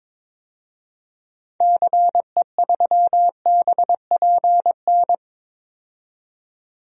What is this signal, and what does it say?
Morse code, heard as a keyed tone, about 21 wpm, 700 Hz. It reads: CE3BPN
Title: DXCC-100 Call Signs - 6 Characters - 22wpm